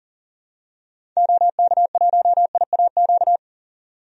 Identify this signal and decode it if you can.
Morse code, heard as a keyed tone, about 39 wpm, 700 Hz. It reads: OK1IAQ
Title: DXCC-100 Call Signs - 6 Characters - 40wpm